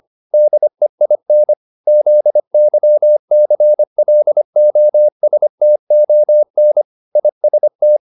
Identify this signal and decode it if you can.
Morse code, heard as a keyed tone, about 25 wpm, 600 Hz. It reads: DEIN ZYCLOSTON IST